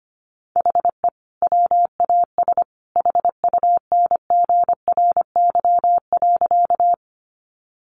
Morse code, heard as a keyed tone, about 25 wpm, 700 Hz. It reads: HE WAS HUNGRY.
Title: Sentences from Top 1400 Words 25wpm